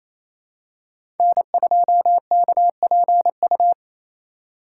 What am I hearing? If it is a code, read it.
Morse code, heard as a keyed tone, about 28 wpm, 700 Hz. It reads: N2KPU
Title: USA Call Signs - 2x Character Spacing - 28wpm